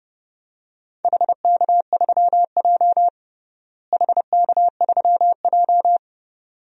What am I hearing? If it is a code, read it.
Morse code, heard as a keyed tone, about 30 wpm, 700 Hz. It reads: HK3J HK3J
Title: DXCC-100 Call Signs - 4 Characters - Repeated 2x - 30wpm